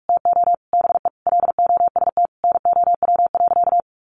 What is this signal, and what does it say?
Morse code, heard as a keyed tone, about 44 wpm, 700 Hz. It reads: TO BE LOST NOW.
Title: Sentences from Top 1400 Words 45wpm